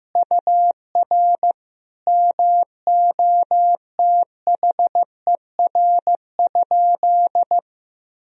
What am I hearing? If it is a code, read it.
Morse code, heard as a keyed tone, about 15 wpm, 700 Hz. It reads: UR MOTHER?
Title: Sentences from Top 200 Words and Abbreviations 15wpm